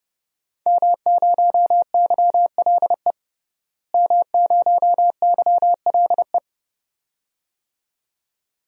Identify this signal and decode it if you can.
Morse code, heard as a keyed tone, about 30 wpm, 700 Hz. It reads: M0YLE M0YLE
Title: DXCC-100 Call Signs - 5 Characters - Repeated 2x - 30wpm